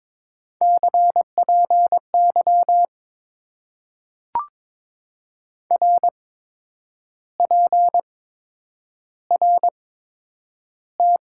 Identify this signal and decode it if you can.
Morse code, heard as a keyed tone, about 22 wpm, spaced as for 9 wpm, 700 Hz. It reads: CPY RPRT
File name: Top 90 QSO Elements - 8x Character Spacing - 22wpm